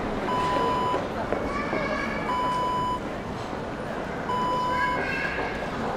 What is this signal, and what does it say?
Speech. There is no code.